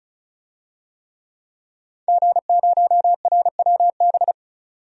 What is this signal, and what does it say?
Morse code, heard as a keyed tone, about 34 wpm, 700 Hz. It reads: G0RWB
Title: DXCC-100 Call Signs - 5 Characters - 35wpm